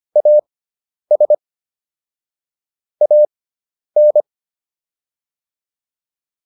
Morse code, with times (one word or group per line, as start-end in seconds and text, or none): 0.2-1.3 AS
3.0-4.2 AN